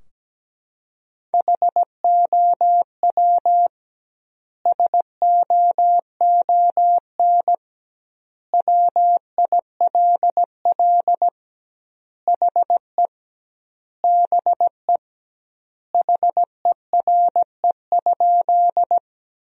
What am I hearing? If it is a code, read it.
Morse code, heard as a keyed tone, about 17 wpm, 700 Hz. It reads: HOW SOON WILL HE BE HERE?
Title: 2x Word-Spacing - Sentences from Top 400 Words - 17wpm